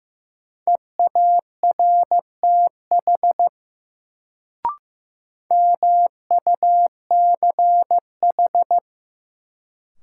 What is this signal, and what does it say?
Morse code, heard as a keyed tone, about 15 wpm, 700 Hz. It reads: EARTH MUCH
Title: Sentences from Top 200 Words - New Word Drill - 15wpm